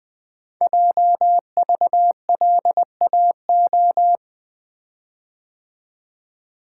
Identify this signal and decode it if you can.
Morse code, heard as a keyed tone, about 21 wpm, 700 Hz. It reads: JVLAO
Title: Code Groups - 5 Letter:Numbers - 20wpm